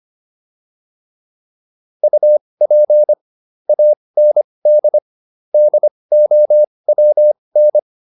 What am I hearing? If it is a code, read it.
Morse code, heard as a keyed tone, about 25 wpm, spaced as for 20 wpm, 600 Hz. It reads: UP AND DOWN
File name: PR303 20